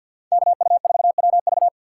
Morse code, heard as a keyed tone, about 46 wpm, 700 Hz. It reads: KU4WV
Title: USA Call Signs - 5 Characters - 6x Character Spacing - 50wpm